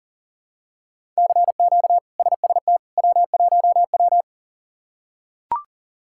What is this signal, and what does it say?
Morse code, heard as a keyed tone, about 37 wpm, 700 Hz. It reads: CQ SST W1W
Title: K1USN SST QSO Practice - 40wpm